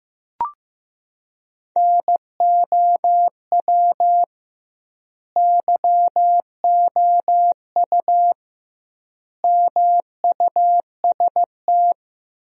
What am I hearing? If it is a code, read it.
Morse code, heard as a keyed tone, about 15 wpm, 700 Hz. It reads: NOW YOU MUST
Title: 2x Word-Spacing - Sentences from Top 300 Words - 15wpm